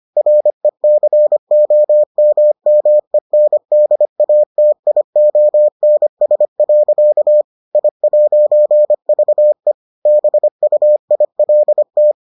Morse code, heard as a keyed tone, about 25 wpm, 600 Hz. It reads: RECOMMENDATIONS. I'VE BUILT